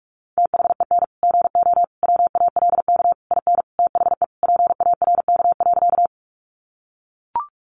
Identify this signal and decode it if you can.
Morse code, heard as a keyed tone, about 44 wpm, 700 Hz. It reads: THEN GO WALK IN THE PARK.
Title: Sentences from Top 1300 Words 45wpm